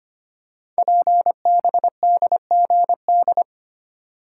Morse code, heard as a keyed tone, about 25 wpm, 700 Hz. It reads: PBDGD
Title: Code Groups - 5 Letters - 25wpm